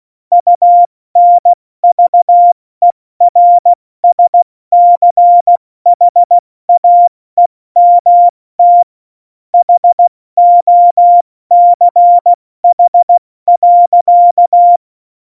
Morse code, heard as a keyed tone, about 16 wpm, 700 Hz. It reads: UNVERSCHAEMT HOCH.
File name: Morsekurs / Witze / Witz 26. Januar 2021, 70 F80